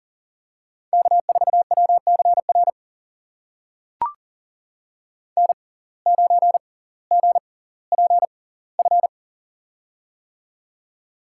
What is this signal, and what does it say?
Morse code, heard as a keyed tone, about 40 wpm, spaced as for 20 wpm, 700 Hz. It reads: K4WCR N9GPF
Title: USA Call Signs - 5 Characters - 6x Character Spacing - 40wpm